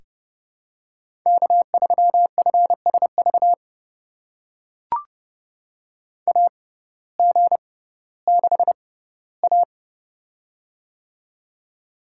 Morse code, heard as a keyed tone, about 32 wpm, spaced as for 17 wpm, 700 Hz. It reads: K3FSV AG6A
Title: USA Call Signs - 6x Character Spacing - 30wpm